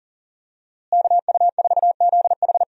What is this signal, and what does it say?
Morse code, heard as a keyed tone, about 40 wpm, 700 Hz. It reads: KU4ZH
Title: USA Call Signs - 5 Characters - Repeated 3x - 40wpm